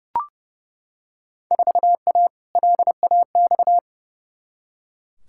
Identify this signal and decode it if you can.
Morse code, heard as a keyed tone, about 30 wpm, 700 Hz. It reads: 4A LAX
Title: Field Day Exchanges - 30wpm